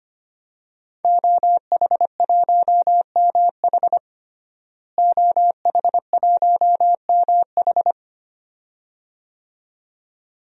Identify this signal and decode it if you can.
Morse code, heard as a keyed tone, about 25 wpm, 700 Hz. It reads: OH1MH OH1MH
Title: DXCC-100 Call Signs - 5 Characters - Repeated 2x - 25wpm